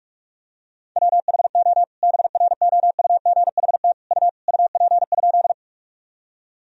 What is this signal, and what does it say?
Morse code, heard as a keyed tone, about 46 wpm, 690 Hz. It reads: WHO BROUGHT U UP?
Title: Sentences from Top 500 Words and Abbreviations 45wpm